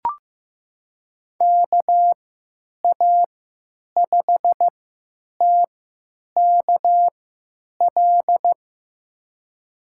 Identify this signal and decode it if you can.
Morse code, heard as a keyed tone, about 15 wpm, spaced as for 9 wpm, 700 Hz. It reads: KA5TKL
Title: USA Call Signs - 6 Characters - 3x Character Spacing - 15wpm